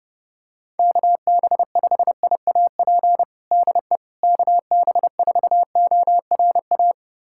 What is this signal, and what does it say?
Morse code, heard as a keyed tone, about 30 wpm, 700 Hz. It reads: KB5IAP DE KB4ORA
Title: USA Callsign DE Callsign - 6 Characters - 30wpm